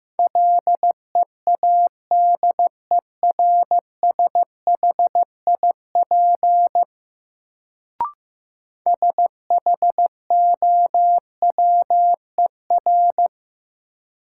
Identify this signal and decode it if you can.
Morse code, heard as a keyed tone, about 15 wpm, 700 Hz. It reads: LEADERSHIP SHOWER
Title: Sentences from Top 1600 Words - New Word Drill - 15wpm